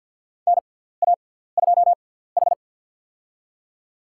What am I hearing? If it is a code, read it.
Morse code, heard as a keyed tone, about 53 wpm, spaced as for 17 wpm, 700 Hz. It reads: NA2H
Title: USA Call Signs - 4 Characters - 6x Character Spacing - 50wpm